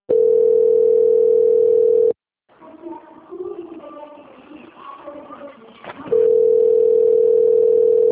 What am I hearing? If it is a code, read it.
Ringback tone (North America), 440 + 480 Hz.
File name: out- - - - - .52377